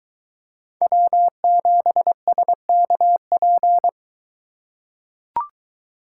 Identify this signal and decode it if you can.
Morse code, heard as a keyed tone, about 22 wpm, 700 Hz. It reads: W7SKP